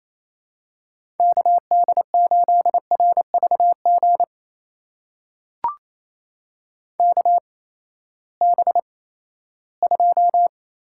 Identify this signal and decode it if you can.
Morse code, heard as a keyed tone, about 28 wpm, spaced as for 16 wpm, 700 Hz. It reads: KD8RVG KB2
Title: USA Call Signs - 6 Characters - 8x Character Spacing - 28wpm